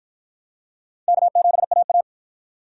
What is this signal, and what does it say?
Morse code, heard as a keyed tone, about 47 wpm, 700 Hz. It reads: K7AA